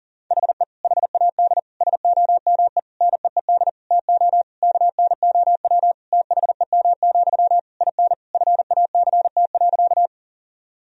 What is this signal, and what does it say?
Morse code, heard as a keyed tone, about 40 wpm, 700 Hz. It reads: HE HAD SOME NEED TO KNOW THEM, IN FACT.
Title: Sentences from Top 1000 Words - Review Part 3 - 40wpm